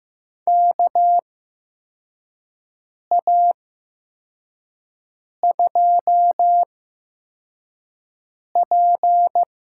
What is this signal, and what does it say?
Morse code, heard as a keyed tone, about 15 wpm, spaced as for 4 wpm, 700 Hz. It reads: KA2P
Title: USA Call Signs - 6 Characters - 8x Character Spacing - 15wpm